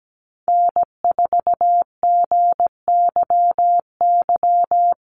Morse code, heard as a keyed tone, about 17 wpm, 700 Hz. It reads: N4GYY
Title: USA Call Signs Repeated 2x 17wpm